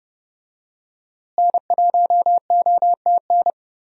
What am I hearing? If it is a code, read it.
Morse code, heard as a keyed tone, about 30 wpm, 700 Hz. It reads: N1OTN
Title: USA Call Signs - 5 Characters - Repeated 2x - 30wpm